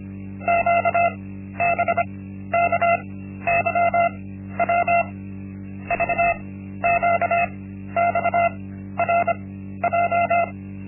Morse code, heard as a keyed tone, about 25 wpm, spaced as for 13 wpm, 690 Hz. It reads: QBKYW VQXRJ
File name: Linksmakalnis-1